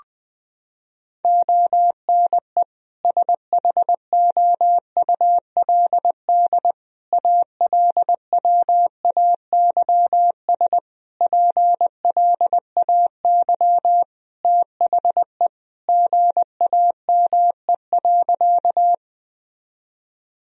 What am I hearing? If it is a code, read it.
Morse code, heard as a keyed tone, about 20 wpm, 700 Hz. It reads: ONE SHOULD ALWAYS PLAY THE GAME.
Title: Sentences from Top 1000 Words - Review Part 3 - 20wpm